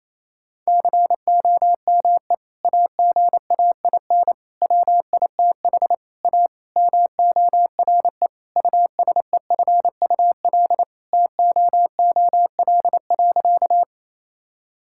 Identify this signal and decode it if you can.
Morse code, heard as a keyed tone, about 28 wpm, 700 Hz. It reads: COME AGAIN WITH A MORE USEFUL TOOL.
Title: Sentences from Top 1000 Words - Review Part 6 - 28wpm